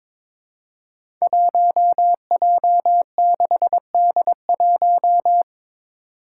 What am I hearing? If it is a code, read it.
Morse code, heard as a keyed tone, about 22 wpm, 700 Hz. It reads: 1J6D1
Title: Five Challenging Letter-Number Combinations 22wpm